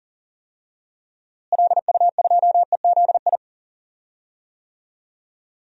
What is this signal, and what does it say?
Morse code, heard as a keyed tone, about 40 wpm, 700 Hz. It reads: LU2EZI
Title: DXCC-100 Call Signs - 6 Characters - Repeated 2x - 40wpm